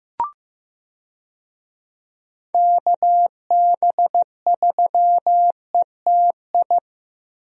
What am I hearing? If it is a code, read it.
Morse code, heard as a keyed tone, about 15 wpm, 700 Hz. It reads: KB3ETI
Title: USA Call Signs - 6 Characters - Repeated 3x - 15wpm